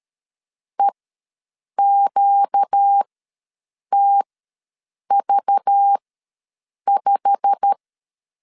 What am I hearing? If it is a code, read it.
Morse code, heard as a keyed tone, about 12 wpm, spaced as for 7 wpm, 800 Hz. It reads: EQTV5
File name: LEZIONI / 9 2867 ws 12wpm 3431